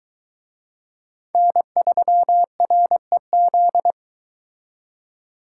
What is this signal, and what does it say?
Morse code, heard as a keyed tone, about 22 wpm, 700 Hz. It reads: N3REZ